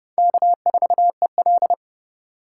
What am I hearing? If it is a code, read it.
Morse code, heard as a keyed tone, about 28 wpm, 700 Hz. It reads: K4EL